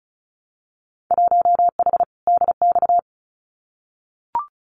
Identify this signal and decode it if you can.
Morse code, heard as a keyed tone, about 36 wpm, 700 Hz. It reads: 1H DX
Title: Winter Field Day Exchanges 35wpm